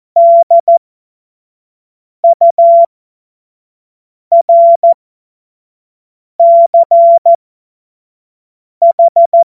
Morse code, heard as a keyed tone, about 14 wpm, spaced as for 5 wpm, 680 Hz. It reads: DURCH
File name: Morsekurs / Witze / Witz 01. März 2023 25 F70